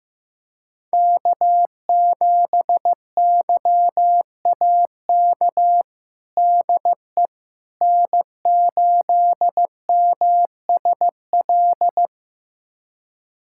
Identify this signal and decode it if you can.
Morse code, heard as a keyed tone, about 15 wpm, 700 Hz. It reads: K7YAK DE N8MSL
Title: USA Callsign DE Callsign - 5 Characters - 15wpm